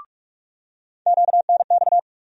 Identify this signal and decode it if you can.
Morse code, heard as a keyed tone, about 45 wpm, 700 Hz. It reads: QNX
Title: USA Call Sign Postfixes - 3 Letters 45wpm